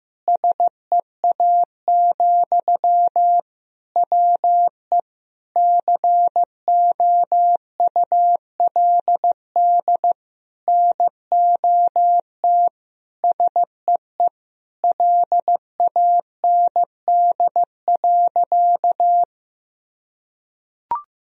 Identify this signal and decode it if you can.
Morse code, heard as a keyed tone, about 15 wpm, 700 Hz. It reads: SEA, WE COULD NOT SEE LAND.
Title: Sentences from Top 1000 Words - Review Part 2 - 15wpm